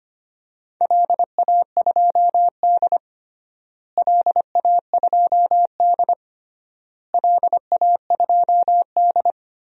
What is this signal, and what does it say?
Morse code, heard as a keyed tone, about 25 wpm, 700 Hz. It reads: LA2D LA2D LA2D
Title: DXCC-100 Call Signs - 4 Characters - Repeated 3x - 25wpm